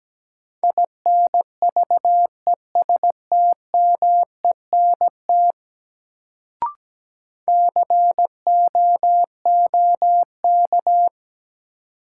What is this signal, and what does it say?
Morse code, heard as a keyed tone, about 17 wpm, 700 Hz. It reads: INVESTMENT COOK
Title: Sentences from Top 700 Words - New Word Drill - 17wpm